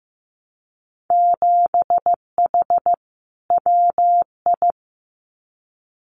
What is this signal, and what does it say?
Morse code, heard as a keyed tone, about 15 wpm, 700 Hz. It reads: 7H WI
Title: Winter Field Day Exchanges Encore 15wpm